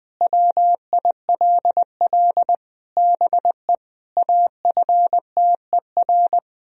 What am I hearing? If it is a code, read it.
Morse code, heard as a keyed tone, about 20 wpm, 700 Hz. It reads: WILL BE AFTER
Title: Sentences from Top 1000 Words - Review Part 1 - 20wpm